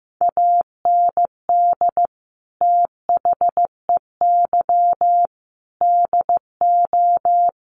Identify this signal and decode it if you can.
Morse code, heard as a keyed tone, about 15 wpm, 700 Hz. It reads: AND THEY DO
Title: Sets of 5 Words Encore2 15wpm